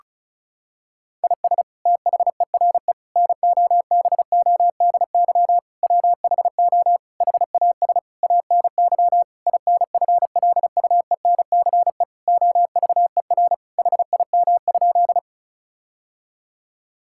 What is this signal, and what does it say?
Morse code, heard as a keyed tone, about 34 wpm, 700 Hz. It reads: IS THERE NOBODY WHO HAS ANY INFLUENCE OVER HIM?
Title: Sentences from Top 1600 Words 35wpm